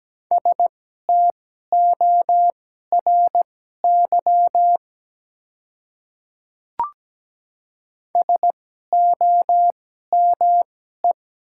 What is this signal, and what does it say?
Morse code, heard as a keyed tone, about 17 wpm, spaced as for 12 wpm, 700 Hz. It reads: STORY SOME
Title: 2x Character Spacing - Top 300 Words - 17wpm